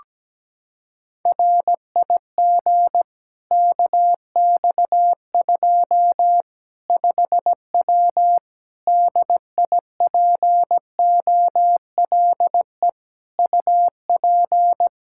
Morse code, heard as a keyed tone, about 17 wpm, 700 Hz. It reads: RIG KX2 5W DIPOLE UP
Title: Practice QSO - Exchange 2 - Info - 17wpm